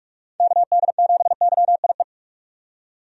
Morse code, heard as a keyed tone, about 43 wpm, 700 Hz. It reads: KD7YIE
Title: USA Call Signs - 6 Characters - 45wpm